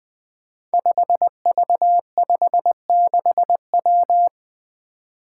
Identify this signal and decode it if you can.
Morse code, heard as a keyed tone, about 20 wpm, 700 Hz. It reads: 5V56W
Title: Five Challenging Letter-Number Combinations 20wpm